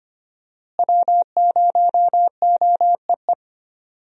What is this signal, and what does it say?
Morse code, heard as a keyed tone, about 25 wpm, 700 Hz. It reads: W0OEE
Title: USA Call Signs - 5 Characters - Repeated 3x - 25wpm